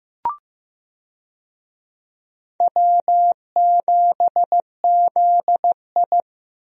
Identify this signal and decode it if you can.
Morse code, heard as a keyed tone, about 15 wpm, 700 Hz. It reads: W7ZI
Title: USA Call Signs - 4 Characters - 15wpm